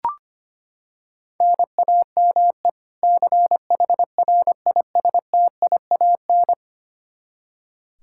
Morse code, heard as a keyed tone, about 25 wpm, 700 Hz. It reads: NAME CHRISTIAN
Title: K1USN SST QSO Practice 25wpm